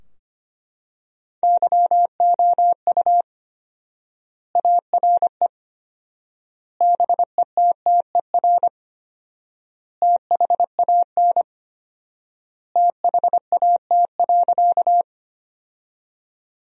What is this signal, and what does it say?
Morse code, heard as a keyed tone, about 25 wpm, 700 Hz. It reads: YOU ARE BETTER THAN THAT.
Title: 4x Word-Spacing - Sentences from Top 400 Words - 25wpm